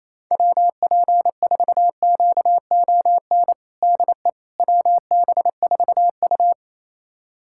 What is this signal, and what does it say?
Morse code, heard as a keyed tone, about 28 wpm, 700 Hz. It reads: WP4QON DE WB4U
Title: USA Callsign DE Callsign 28wpm